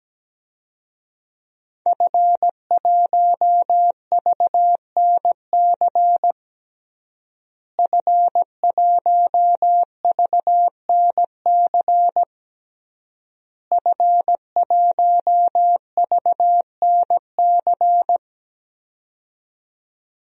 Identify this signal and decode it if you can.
Morse code, heard as a keyed tone, about 17 wpm, 700 Hz. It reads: F1VNC F1VNC F1VNC
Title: DXCC-100 Call Signs - 5 Characters - Repeated 3x - 17wpm